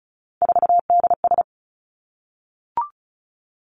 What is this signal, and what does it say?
Morse code, heard as a keyed tone, about 36 wpm, 700 Hz. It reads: 4DS
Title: Three Challenging Letter-Number Combinations 35wpm